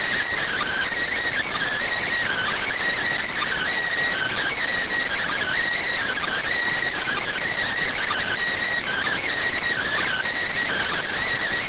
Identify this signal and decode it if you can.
Emergency Alert System (SAME) tones (header bursts), heard as harsh buzzing bursts. It is heard from 0.4 to 11.2 s.